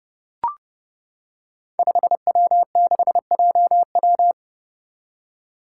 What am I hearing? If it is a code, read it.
Morse code, heard as a keyed tone, about 30 wpm, 700 Hz. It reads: 5W6JW
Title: Five Challenging Letter-Number Combinations 30wpm